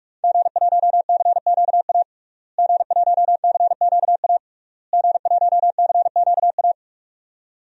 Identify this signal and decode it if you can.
Morse code, heard as a keyed tone, about 46 wpm, 700 Hz. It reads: G1CQA G1CQA G1CQA
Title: DXCC-100 Call Signs - 5 Characters - Repeated 3x - 45wpm